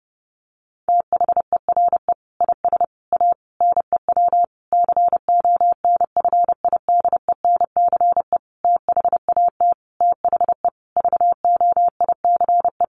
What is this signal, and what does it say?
Morse code, heard as a keyed tone, about 28 wpm, 700 Hz. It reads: THERE IS A NEW CONFIDENCE THAT THE VOICE
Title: Sentences from Top 1500 Words 30wpm